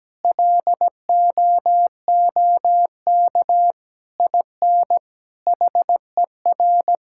Morse code, heard as a keyed tone, about 17 wpm, 700 Hz. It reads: LOOK IN HER
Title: Sentences from Top 1000 Words - Review Part 5 - 17wpm